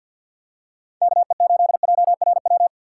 Morse code, heard as a keyed tone, about 52 wpm, 700 Hz. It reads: KE8JRW
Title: USA Call Signs - 6 Characters - 3x Character Spacing - 50wpm